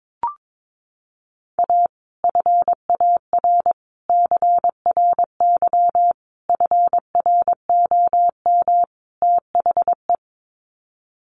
Morse code, heard as a keyed tone, about 22 wpm, 700 Hz. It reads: A FAR CRY FROM THE
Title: Sets of 5 Words Encore 22wpm